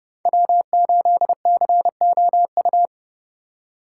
Morse code, heard as a keyed tone, about 29 wpm, 700 Hz. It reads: W8COU